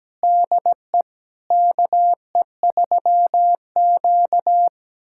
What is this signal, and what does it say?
Morse code, heard as a keyed tone, about 17 wpm, 700 Hz. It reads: DE KE3Q
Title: USA Callsign DE Callsign - 4 Characters - 17wpm